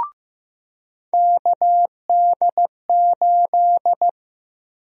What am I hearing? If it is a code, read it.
Morse code, heard as a keyed tone, about 16 wpm, 700 Hz. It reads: KD8
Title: USA Call Sign Prefixes - Repeated 2x - 15wpm